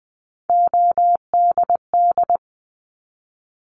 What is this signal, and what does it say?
Morse code, heard as a keyed tone, about 20 wpm, 700 Hz. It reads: ODD
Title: 260 Lesson 42e - Words 20-20wpm